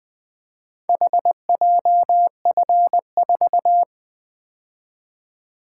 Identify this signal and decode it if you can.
Morse code, heard as a keyed tone, about 20 wpm, 700 Hz. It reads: HJF4
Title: Four Letter-Number Combinations - 20wpm